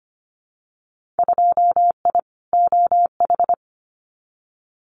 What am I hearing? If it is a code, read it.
Morse code, heard as a keyed tone, about 25 wpm, 700 Hz. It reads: 2I OH
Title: Winter Field Day Exchanges 25wpm